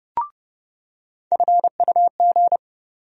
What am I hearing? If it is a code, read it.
Morse code, heard as a keyed tone, about 30 wpm, 700 Hz. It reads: FUG
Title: USA Call Sign Postfixes - 3 Letters 30wpm